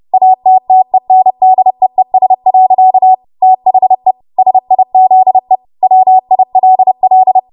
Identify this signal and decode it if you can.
Morse code, heard as a keyed tone, about 30 wpm, 750 Hz. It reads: ATTENDEES. THE SIZE WILL
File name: Morse / 30WPM